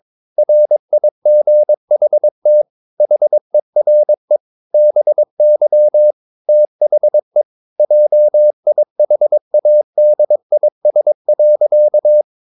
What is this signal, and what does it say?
Morse code, heard as a keyed tone, about 22 wpm, 600 Hz. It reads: RIGHT HERE BY THE JIHADIS.
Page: Romanen Never af Ken Follet på engelsk.